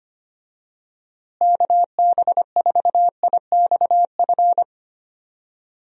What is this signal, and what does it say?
Morse code, heard as a keyed tone, about 25 wpm, 700 Hz. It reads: KB4IXF
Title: USA Call Signs - 6 Characters - Repeated 3x - 25wpm